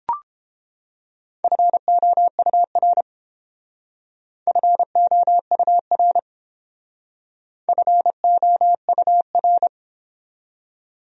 Morse code, heard as a keyed tone, about 29 wpm, 700 Hz. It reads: FOUR FOUR FOUR